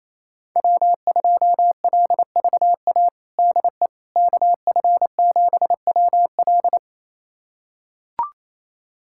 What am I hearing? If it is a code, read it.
Morse code, heard as a keyed tone, about 27 wpm, 700 Hz. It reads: W2LVA DE KF7WL
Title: USA Callsign DE Callsign - 5 Characters - 28wpm